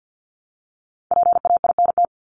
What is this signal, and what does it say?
Morse code, heard as a keyed tone, about 49 wpm, 700 Hz. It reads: PAINT
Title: ING Suffix - Top 100 Words with Base Words 50wpm